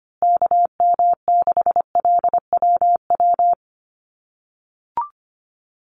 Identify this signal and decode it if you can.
Morse code, heard as a keyed tone, about 25 wpm, 700 Hz. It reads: KM6LWW
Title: USA Call Signs Repeated 2x 25wpm